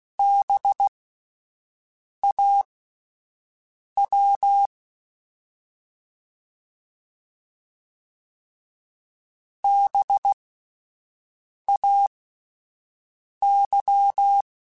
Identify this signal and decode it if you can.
Morse code, heard as a keyed tone, about 16 wpm, spaced as for 5 wpm, 800 Hz. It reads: BAW BAY
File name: Spaced Alphas / Spaced Alphas mp3s / Bs sp @10